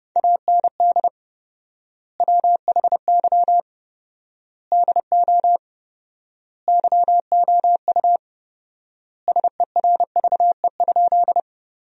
Morse code, heard as a keyed tone, about 32 wpm, 700 Hz. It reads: AND WHY DO YOU SERVE?
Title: 4x Word-Spacing - Sentences from Top 400 Words - 30wpm